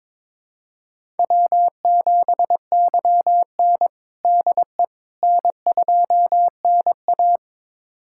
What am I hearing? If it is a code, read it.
Morse code, heard as a keyed tone, about 22 wpm, 700 Hz. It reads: W7YN DE N2NA
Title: USA Callsign DE Callsign - 4 Characters - 22wpm